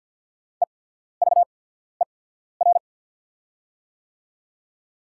Morse code, heard as a keyed tone, about 48 wpm, spaced as for 14 wpm, 690 Hz. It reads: EVER
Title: 8x Character Spacing - Top 500 Words - 50wpm